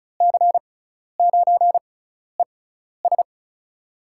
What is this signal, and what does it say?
Morse code, heard as a keyed tone, about 34 wpm, spaced as for 12 wpm, 700 Hz. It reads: C9ES
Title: USA Call Signs - 5 Characters - 6x Character Spacing - 35wpm